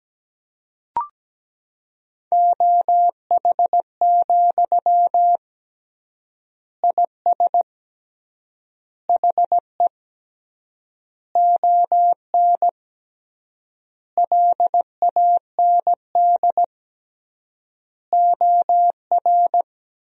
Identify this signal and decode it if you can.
Morse code, heard as a keyed tone, about 17 wpm, 700 Hz. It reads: OH, IS HE ON LAND OR
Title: 3x Word-Spacing - Sentences from Top 500 Words - 17wpm